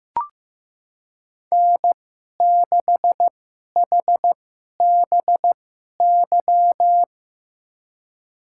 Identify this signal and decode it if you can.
Morse code, heard as a keyed tone, about 15 wpm, spaced as for 11 wpm, 700 Hz. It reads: N6HBY
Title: USA Call Signs - 5 Characters - 2x Character Spacing - 15wpm